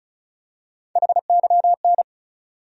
Morse code, heard as a keyed tone, about 36 wpm, 700 Hz. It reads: HYN
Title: USA Call Sign Postfixes - 3 Letters 35wpm